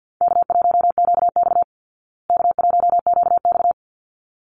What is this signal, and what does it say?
Morse code, heard as a keyed tone, about 50 wpm, 700 Hz. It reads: K1QX K1QX
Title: Easy Call Signs Repeated 2x - 50wpm